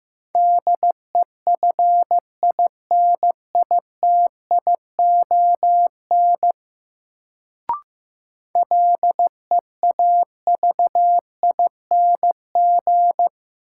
Morse code, heard as a keyed tone, about 15 wpm, 700 Hz. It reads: DEFINITION LEAVING
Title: Sentences from Top 1000 Words - Review Part 4 - New Word Drill - 15wpm